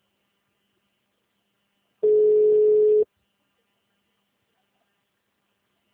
Ringback tone (Europe, 425 Hz).